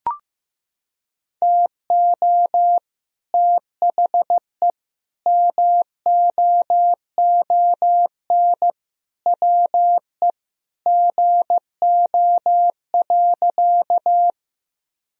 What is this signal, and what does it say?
Morse code, heard as a keyed tone, about 15 wpm, 700 Hz. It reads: TO THE MOON WE GO.
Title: Sentences from Top 1000 Words - Review Part 3 - 15wpm